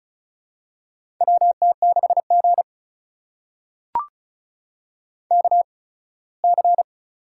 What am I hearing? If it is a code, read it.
Morse code, heard as a keyed tone, about 33 wpm, spaced as for 21 wpm, 700 Hz. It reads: WT6G KC
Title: USA Call Signs - 4 Characters - 8x Character Spacing - 35wpm